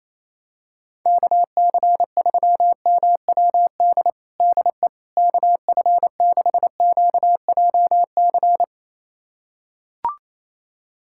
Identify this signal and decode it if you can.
Morse code, heard as a keyed tone, about 28 wpm, 700 Hz. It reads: KC3MWD DE KF6QJC
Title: USA Callsign DE Callsign 28wpm